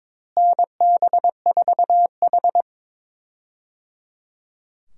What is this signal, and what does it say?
Morse code, heard as a keyed tone, about 22 wpm, 700 Hz. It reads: NB4H
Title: USA Call Signs - 4 Characters - Repeated 3x - 22wpm